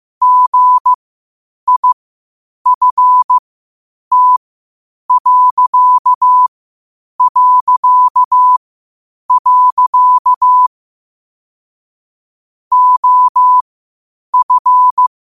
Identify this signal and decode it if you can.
Morse code, heard as a keyed tone, about 15 wpm, spaced as for 8 wpm, 1000 Hz. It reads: GIFT... OF
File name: qotdfiles / 8wpm